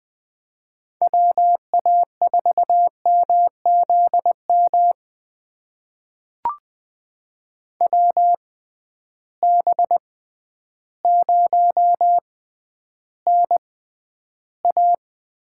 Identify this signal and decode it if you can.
Morse code, heard as a keyed tone, about 20 wpm, spaced as for 11 wpm, 700 Hz. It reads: WA4MZM WB0NA
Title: USA Call Signs - 6 Characters - 6x Character Spacing - 20wpm